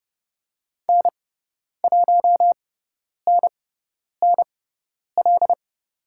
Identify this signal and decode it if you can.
Morse code, heard as a keyed tone, about 30 wpm, spaced as for 10 wpm, 700 Hz. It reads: N1NNL
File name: Lesson 16g - Letter c - Call Signs 30wpm